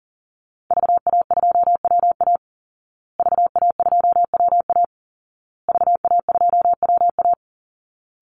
Morse code, heard as a keyed tone, about 40 wpm, 700 Hz. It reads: VA2WA VA2WA VA2WA
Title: Call Signs Repeated 2x - 40wpm